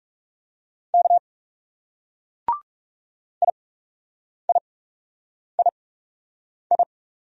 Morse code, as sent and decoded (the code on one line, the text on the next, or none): -.- / .. .. .. ..
K IIII